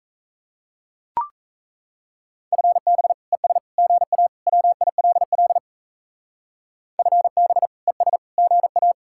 Morse code, heard as a keyed tone, about 34 wpm, 700 Hz. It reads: FB ES GA WILL FB ES GA
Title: Speed-Racing Common QSO Phrases - 28wpm